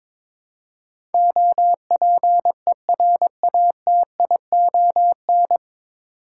Morse code, heard as a keyed tone, about 22 wpm, 700 Hz. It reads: OPERATION
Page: Morse Code Ninja practice 22wpm